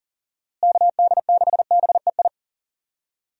Morse code, heard as a keyed tone, about 40 wpm, 700 Hz. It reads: KD6BEI
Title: USA Call Signs - 6 Characters - Repeated 3x - 40wpm